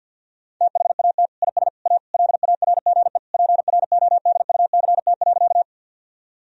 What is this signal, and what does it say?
Morse code, heard as a keyed tone, about 52 wpm, 700 Hz. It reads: THAT IS A LARGE PRODUCT.
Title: Sentences from Top 1000 Words - Review Part 2 - 50wpm